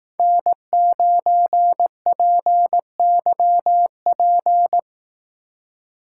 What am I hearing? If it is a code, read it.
Morse code, heard as a keyed tone, about 18 wpm, 700 Hz. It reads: N9PYP
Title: Speed Racing - USA 1x3 Call Signs 15wpm